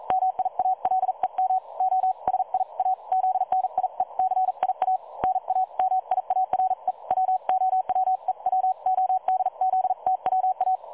Morse code, heard as a keyed tone, about 41 wpm, spaced as for 36 wpm, 750 Hz. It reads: MIALEM OSIAZNIECIA NA MIARE WOWEWODZTWA
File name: dlazaawansowanych / nagraniazpasma / websdr recording 3565.0kHz